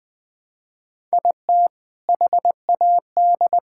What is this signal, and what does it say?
Morse code, heard as a keyed tone, about 20 wpm, 700 Hz. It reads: IT HAD
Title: Sentences with 5-Word Phrases - Spoken Beginning - 20wpm